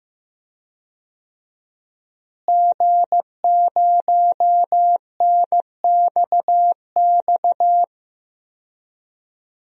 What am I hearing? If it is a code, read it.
Morse code, heard as a keyed tone, about 15 wpm, 700 Hz. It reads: G0NXX
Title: DXCC-100 Call Signs - 5 Characters - Repeated 2x - 15wpm